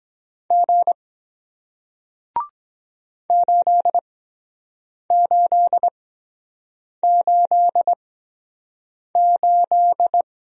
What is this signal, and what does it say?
Morse code, heard as a keyed tone, about 23 wpm, spaced as for 7 wpm, 700 Hz. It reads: G 8888